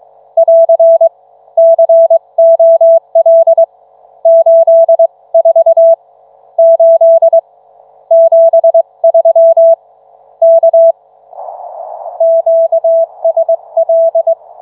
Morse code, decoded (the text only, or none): <AR> COL 84 8 73 K QSL